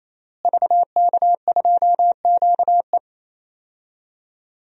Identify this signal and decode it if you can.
Morse code, heard as a keyed tone, about 28 wpm, 700 Hz. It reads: VK2QE
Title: DXCC-100 Call Signs - 5 Characters - Repeated 3x - 28wpm